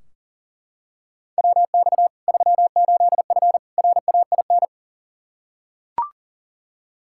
Morse code, heard as a keyed tone, about 40 wpm, 700 Hz. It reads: WX 38F RAIN
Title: Practice QSO - Exchange 2 - Info - 40wpm